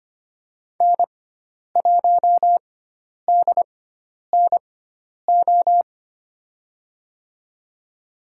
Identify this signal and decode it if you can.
Morse code, heard as a keyed tone, about 25 wpm, spaced as for 10 wpm, 700 Hz. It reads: N1DNO
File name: Lesson 16g - Letter c - Call Signs 25wpm